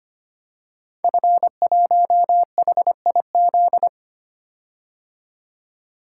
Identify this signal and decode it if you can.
Morse code, heard as a keyed tone, about 25 wpm, 700 Hz. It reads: F1HIZ
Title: Code Groups - 5 Letter:Numbers - 25wpm